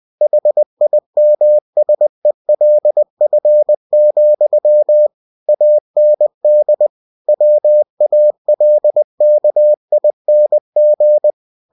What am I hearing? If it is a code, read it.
Morse code, heard as a keyed tone, about 20 wpm, 600 Hz. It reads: HIMSELF, AND WALKING
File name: cw-books / winnie-the-pooh / winnie-20wpm-page-0013